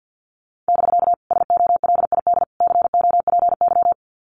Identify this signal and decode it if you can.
Morse code, heard as a keyed tone, about 50 wpm, 700 Hz. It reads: <BK> SOLID COPY
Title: Common QSO Phrases Encore 50wpm